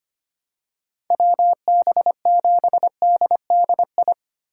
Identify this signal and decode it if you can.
Morse code, heard as a keyed tone, about 25 wpm, 700 Hz. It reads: WB7DDI
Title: USA Call Signs - 2x Character Spacing - 25wpm